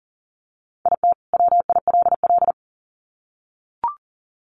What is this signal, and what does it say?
Morse code, heard as a keyed tone, about 40 wpm, 700 Hz. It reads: IT WILL
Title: Sets of 2 Words 40wpm